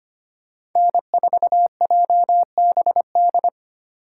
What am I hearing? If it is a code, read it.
Morse code, heard as a keyed tone, about 25 wpm, 700 Hz. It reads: N4JBD
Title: USA Call Signs - 5 Characters - 4x Character Spacing - 25wpm